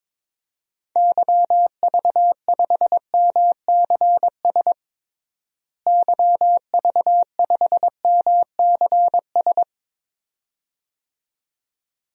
Morse code, heard as a keyed tone, about 21 wpm, 700 Hz. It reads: YV5MCS YV5MCS
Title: DXCC-100 Call Signs - 6 Characters - Repeated 2x - 22wpm